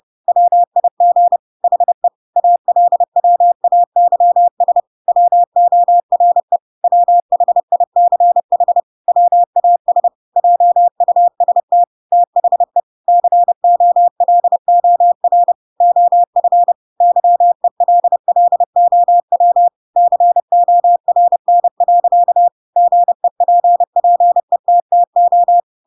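Morse code, heard as a keyed tone, about 30 wpm, 700 Hz. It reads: WIG HE ALWAYS WORE WHICH WAS JUST THE COLOR OF YELLOW CORN. GEPPETTO
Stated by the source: Chapter 2 of The Adventures of Pinocchio read as a book in Morse code (CW) at 30 wpm, by ik1hge.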